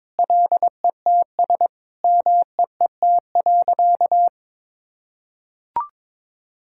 Morse code, heard as a keyed tone, about 22 wpm, 700 Hz. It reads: LETS MEET.
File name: practice / 22wpm / Sentences with 3-Word Phrases - 22wpm